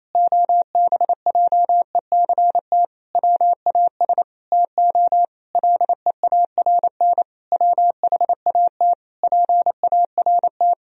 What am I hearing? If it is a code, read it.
Morse code, heard as a keyed tone, about 28 wpm, 700 Hz. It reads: OBJECT WAS TO LEARN WHAT PART
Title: Sentences from Top 1000 Words - Review Part 3 - 28wpm